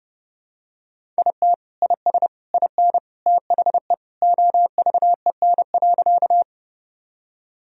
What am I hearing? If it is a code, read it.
Morse code, heard as a keyed tone, about 30 wpm, 700 Hz. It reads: IT IS IN THE OVEN.
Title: Sentences from Top 1000 Words - Review Part 4 - 30wpm